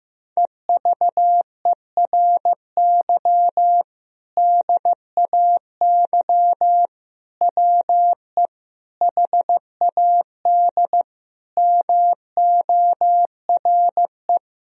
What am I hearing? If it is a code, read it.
Morse code, heard as a keyed tone, about 15 wpm, 700 Hz. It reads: EVERY DAY WE HAD MORE
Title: Sentences from Top 1000 Words - Review Part 3 - 15wpm